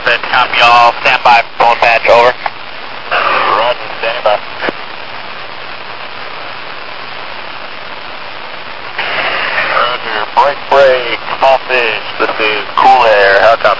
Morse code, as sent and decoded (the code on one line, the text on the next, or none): none